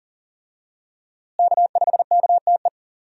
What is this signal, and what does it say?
Morse code, heard as a keyed tone, about 39 wpm, 700 Hz. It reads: K5KTE
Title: USA Call Signs - 5 Characters - Repeated 3x - 40wpm